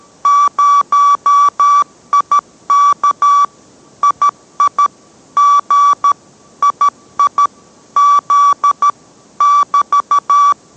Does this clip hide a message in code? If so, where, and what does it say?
Yes: Morse code from 0.3 to 10.5 s, heard as a keyed tone, about 13 wpm, 1150 Hz. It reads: 0IK II G II Z <BT>